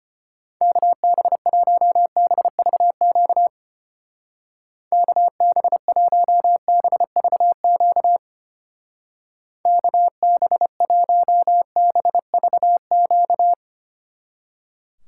Morse code, heard as a keyed tone, about 32 wpm, 700 Hz. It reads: KB1BVQ KB1BVQ KB1BVQ